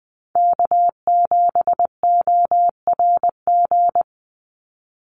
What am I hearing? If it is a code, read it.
Morse code, heard as a keyed tone, about 20 wpm, 700 Hz. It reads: K7ORG
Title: NA SOTA Chaser Callsigns 20wpm